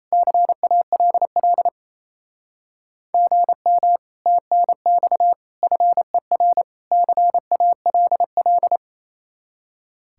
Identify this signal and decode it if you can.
Morse code, heard as a keyed tone, about 30 wpm, 700 Hz. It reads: CALL GM TNX FER CALL
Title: Speed-Racing Common QSO Phrases - 28wpm